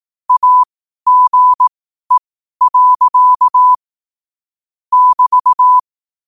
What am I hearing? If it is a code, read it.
Morse code, heard as a keyed tone, about 18 wpm, spaced as for 12 wpm, 1000 Hz. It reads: AGE. <BT>
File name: qotdfiles / 12wpm